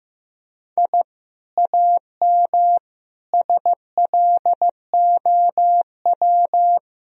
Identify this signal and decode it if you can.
Morse code, heard as a keyed tone, about 15 wpm, 700 Hz. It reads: I AM SLOW
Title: Sentences from Top 1000 Words - Review Part 2 - 15wpm